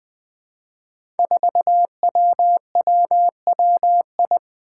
Morse code, heard as a keyed tone, about 20 wpm, 700 Hz. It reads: 4WWWI
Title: Five Challenging Letter-Number Combinations 20wpm